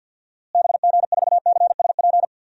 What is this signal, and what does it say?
Morse code, heard as a keyed tone, about 47 wpm, 700 Hz. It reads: BG4CSP
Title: DXCC-100 Call Signs - 6 Characters - Repeated 3x - 50wpm